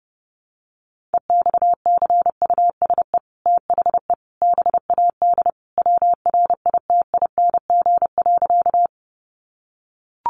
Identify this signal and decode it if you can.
Morse code, heard as a keyed tone, about 30 wpm, 700 Hz. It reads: EXCUSE THE BAD WRITING.
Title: Sentences from Top 800 Words 30wpm